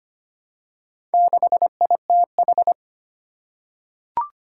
Morse code, heard as a keyed tone, about 25 wpm, 700 Hz. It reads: 6ITH
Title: Four Challenging Letter-Number Combinations 25wpm